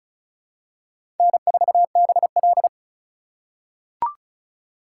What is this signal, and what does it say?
Morse code, heard as a keyed tone, about 34 wpm, 700 Hz. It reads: N4BL
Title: USA Call Signs - 4 Characters - 2x Character Spacing - 35wpm